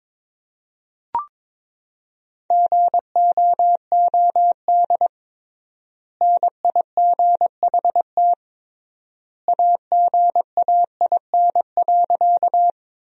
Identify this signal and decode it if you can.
Morse code, heard as a keyed tone, about 22 wpm, 700 Hz. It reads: GOOD NIGHT AGAIN.
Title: 3x Word-Spacing - Sentences from Top 300 Words - 22wpm